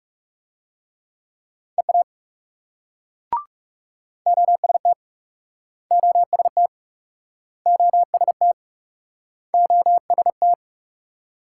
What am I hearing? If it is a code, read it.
Morse code, heard as a keyed tone, about 37 wpm, 700 Hz. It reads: EA OST OST OST OST
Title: Speed-Racing Multi-letter Phonemes - 30wpm